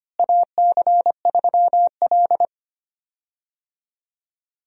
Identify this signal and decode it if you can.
Morse code, heard as a keyed tone, about 25 wpm, 700 Hz. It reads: AC3L
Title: USA Call Signs - 4 Characters - Repeated 3x - 25wpm